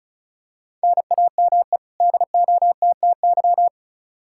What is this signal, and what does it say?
Morse code, heard as a keyed tone, about 34 wpm, 700 Hz. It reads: NAME DOTTY
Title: K1USN SST QSO Practice 35wpm